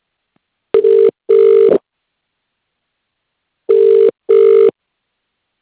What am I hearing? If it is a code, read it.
Ringback tone (UK), 400 + 450 Hz.